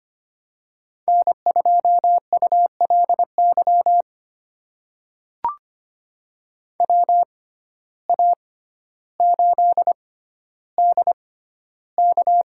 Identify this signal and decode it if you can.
Morse code, heard as a keyed tone, about 25 wpm, spaced as for 13 wpm, 700 Hz. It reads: N2ULY WA8DK
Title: USA Call Signs - 6x Character Spacing - 25wpm